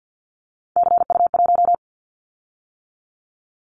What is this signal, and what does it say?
Morse code, heard as a keyed tone, about 50 wpm, 700 Hz. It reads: CU1
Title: Callsign Prefixes for Top 100 DXCC - 50wpm